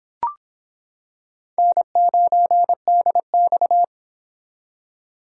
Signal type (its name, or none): Morse code